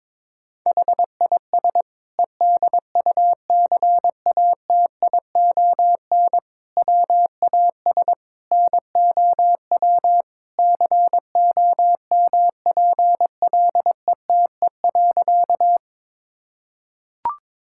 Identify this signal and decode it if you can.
Morse code, heard as a keyed tone, about 22 wpm, 700 Hz. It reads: HIS EDUCATION WAS NOW COMPLETE.
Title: Sentences from Top 1000 Words - Review Part 6 - 22wpm